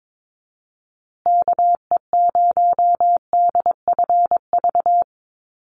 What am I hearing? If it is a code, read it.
Morse code, heard as a keyed tone, about 22 wpm, 700 Hz. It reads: KE0DFV
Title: USA Call Signs Repeated 3x 22wpm